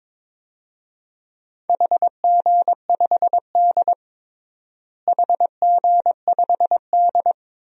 Morse code, heard as a keyed tone, about 22 wpm, 700 Hz. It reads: HG5D HG5D
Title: DXCC-100 Call Signs - 4 Characters - Repeated 2x - 22wpm